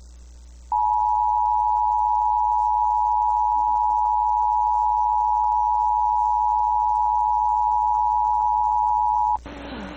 Emergency Alert System (SAME) tones (the 853 and 960 Hz attention tone). It is heard from 0.7 to 9.4 s.